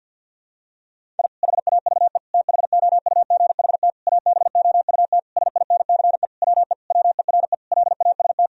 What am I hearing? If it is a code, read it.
Morse code, heard as a keyed tone, about 50 wpm, 700 Hz. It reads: I HAVE THOUGHT ABOUT SINCE WE WERE LAST
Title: Sentences from Top 1000 Words - Review Part 1 - 50wpm